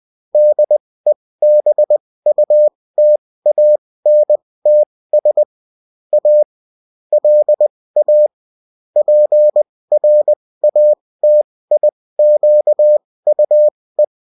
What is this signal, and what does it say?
Morse code, heard as a keyed tone, about 20 wpm, spaced as for 16 wpm, 600 Hz. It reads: DEBUTANTS A LA PRATIQUE